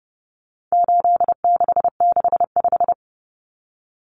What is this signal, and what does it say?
Morse code, heard as a keyed tone, about 30 wpm, 700 Hz. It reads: 8665
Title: Numbers - 4-Digit 30wpm